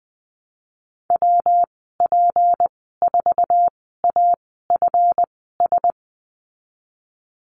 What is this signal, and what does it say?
Morse code, heard as a keyed tone, about 20 wpm, spaced as for 15 wpm, 700 Hz. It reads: WP4AFS
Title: 161 Lesson 26g - Letter g - Call Signs 20wpm